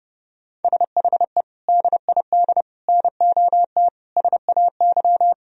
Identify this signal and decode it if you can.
Morse code, heard as a keyed tone, about 32 wpm, 700 Hz. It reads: SHE DID NOT SAY
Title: Sentences from Top 1000 Words - Review Part 3 - 30wpm